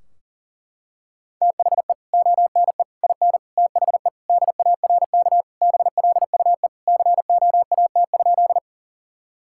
Morse code, heard as a keyed tone, about 40 wpm, 700 Hz. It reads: THE ONE IN THE DARK BLUE COAT?
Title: Sentences from Top 1600 Words 40wpm